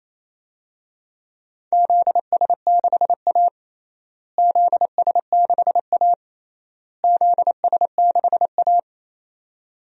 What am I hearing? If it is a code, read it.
Morse code, heard as a keyed tone, about 28 wpm, 700 Hz. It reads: ZS6A ZS6A ZS6A
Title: DXCC-100 Call Signs - 4 Characters - Repeated 3x - 28wpm